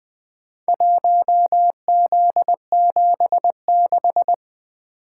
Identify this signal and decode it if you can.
Morse code, heard as a keyed tone, about 20 wpm, 700 Hz. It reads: 1Z76
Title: Four Letter-Number Combinations - 20wpm